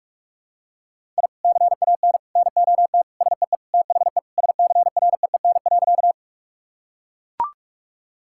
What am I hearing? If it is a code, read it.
Morse code, heard as a keyed tone, about 44 wpm, 700 Hz. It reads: I CAN NOT SEE THE SCREEN.
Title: Sentences from Top 1000 Words - Review Part 6 - 45wpm